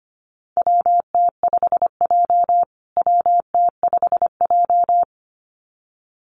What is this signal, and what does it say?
Morse code, heard as a keyed tone, about 25 wpm, 700 Hz. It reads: WT5J WT5J
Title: USA Call Signs Repeated 2x 25wpm